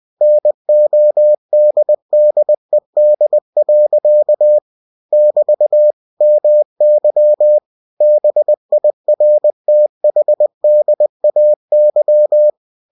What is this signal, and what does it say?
Morse code, heard as a keyed tone, about 20 wpm, 600 Hz. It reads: NODDED. <BT> MY BIRTHDAY
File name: cw-books / winnie-the-pooh / winnie-20wpm-page-0056